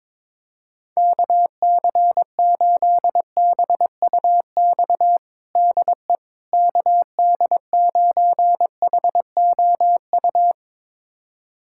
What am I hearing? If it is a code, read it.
Morse code, heard as a keyed tone, about 22 wpm, 700 Hz. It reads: KC8BUX DE KD9HOU
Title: USA Callsign DE Callsign - 6 Characters - 22wpm